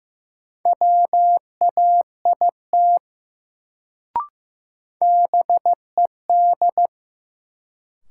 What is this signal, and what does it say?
Morse code, heard as a keyed tone, about 15 wpm, 700 Hz. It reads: WAIT BED
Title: Sentences from Top 500 Words - New Word Drill - 15wpm